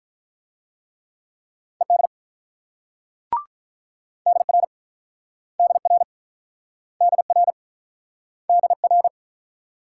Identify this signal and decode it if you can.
Morse code, heard as a keyed tone, about 49 wpm, 700 Hz. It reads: ED DR DR DR DR